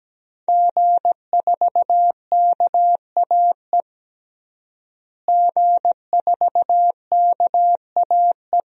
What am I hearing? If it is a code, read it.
Morse code, heard as a keyed tone, about 17 wpm, 700 Hz. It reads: G4KAE G4KAE
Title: DXCC-100 Call Signs - 5 Characters - Repeated 2x - 17wpm